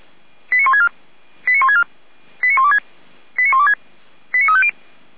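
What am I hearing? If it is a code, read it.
Five-tone selective call (ZVEI), heard as quick runs of five tones, tones of about 70 ms. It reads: 89164 89164 89126 89126 89350
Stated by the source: ZVEI five-tone selective call received on SDR.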